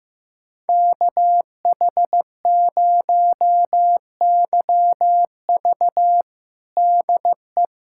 Morse code, heard as a keyed tone, about 15 wpm, 700 Hz. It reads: KH0YV DE
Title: USA Callsign DE Callsign - 5 Characters - 15wpm